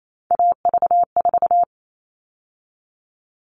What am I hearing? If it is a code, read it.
Morse code, heard as a keyed tone, about 28 wpm, 700 Hz. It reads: AV4
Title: All Cut Numbers - 3-Digit 28wpm